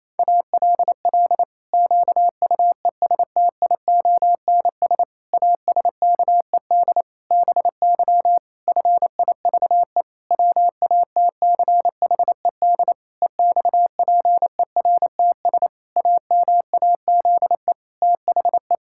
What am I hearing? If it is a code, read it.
Morse code, heard as a keyed tone, about 28 wpm, 700 Hz. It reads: ALL QUESTIONS ASKED BY FIVE WATCHED EXPERTS AMAZE THE
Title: Pangram Challenge v2 - 28wpm